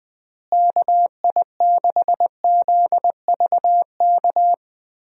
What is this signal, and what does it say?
Morse code, heard as a keyed tone, about 20 wpm, 700 Hz. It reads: KI6ZVK
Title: USA Call Signs - 6 Characters - Repeated 3x - 20wpm